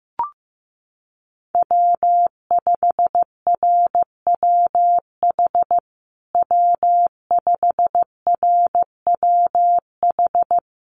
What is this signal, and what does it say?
Morse code, heard as a keyed tone, about 15 wpm, 700 Hz. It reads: W5RWH W5RWH
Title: USA Call Signs Repeated 2x 15wpm